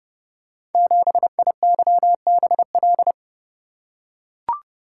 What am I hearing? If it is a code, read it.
Morse code, heard as a keyed tone, about 31 wpm, 700 Hz. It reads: 7IYBL
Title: Code Groups - 5 Letter:Numbers - 30wpm